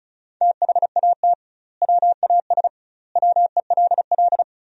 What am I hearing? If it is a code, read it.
Morse code, heard as a keyed tone, about 34 wpm, 700 Hz. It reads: THAT WAS WELL
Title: 2x Word-Spacing - Sentences from Top 500 Words - 35wpm